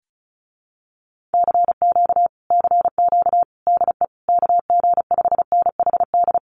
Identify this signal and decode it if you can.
Morse code, heard as a keyed tone, about 34 wpm, 700 Hz. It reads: CQ CQ DE KG5NHD
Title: Common QSO Phrases Encore 35wpm